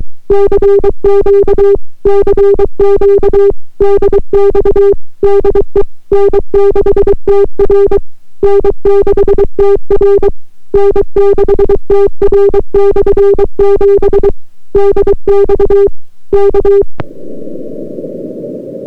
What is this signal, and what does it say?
Morse code, heard as a keyed tone, about 23 wpm, 400 Hz. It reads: CQ CQ DX DE N6TR N6TR N6TR/7 DX K